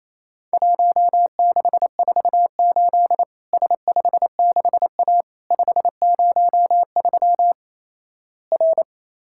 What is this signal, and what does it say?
Morse code, heard as a keyed tone, about 28 wpm, 700 Hz. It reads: 1648 S56A 503 R
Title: Worked All Europe DX Contest - QTC Exchange - 28wpm